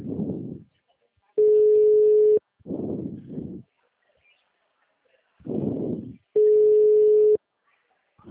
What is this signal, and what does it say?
Ringback tone (Europe, 425 Hz).